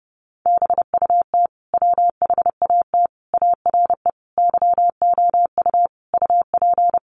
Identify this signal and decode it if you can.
Morse code, heard as a keyed tone, about 30 wpm, 700 Hz. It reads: BUT WHAT ARE YOU UP
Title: Sentences from Top 100 Words 30wpm